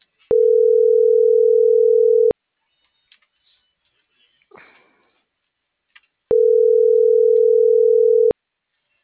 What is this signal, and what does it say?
Ringback tone (North America), 440 + 480 Hz.